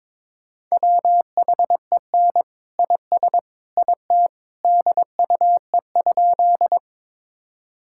Morse code, heard as a keyed tone, about 22 wpm, 700 Hz. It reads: WHEN IS IT DUE?
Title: Sentences from Top 1000 Words - Review Part 6 - 22wpm